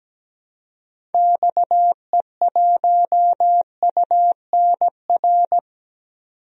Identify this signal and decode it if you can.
Morse code, heard as a keyed tone, about 17 wpm, 700 Hz. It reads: XE1UNR
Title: DXCC-100 Call Signs - 6 Characters - 17wpm